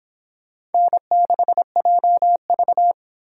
Morse code, heard as a keyed tone, about 29 wpm, 700 Hz. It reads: N6JV